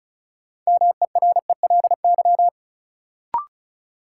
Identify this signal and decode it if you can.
Morse code, heard as a keyed tone, about 36 wpm, 700 Hz. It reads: MERELY
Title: Sentences from Top 1400 Words - New Word Drill - 35wpm